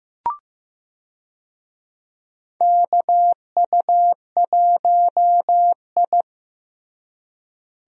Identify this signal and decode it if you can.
Morse code, heard as a keyed tone, about 15 wpm, 700 Hz. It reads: KU1I
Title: USA Call Signs - 4 Characters - Repeated 2x - 15wpm